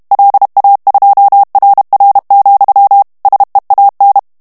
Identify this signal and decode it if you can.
Morse code, heard as a keyed tone, about 32 wpm, 800 Hz. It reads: LA2RR, SEAN
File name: morse mp3 / ESPAGNOL TEXTE EN CLAIR 1 VITESSE 30 MOTS MINUTES